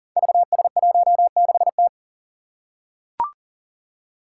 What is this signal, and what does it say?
Morse code, heard as a keyed tone, about 40 wpm, 700 Hz. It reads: VS16T
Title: Five Challenging Letter-Number Combinations 40wpm